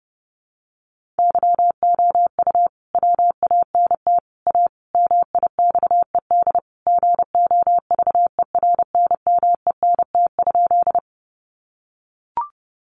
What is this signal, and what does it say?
Morse code, heard as a keyed tone, about 30 wpm, 700 Hz. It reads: YOU WANT A MIXED GOVERNMENT?
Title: Sentences from Top 1500 Words 30wpm